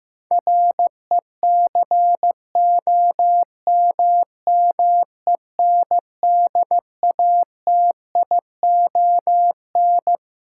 Morse code, heard as a keyed tone, about 15 wpm, 700 Hz. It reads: RECOMMENDATION
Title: Sentences from Top 1400 Words - New Word Drill Alt - 15wpm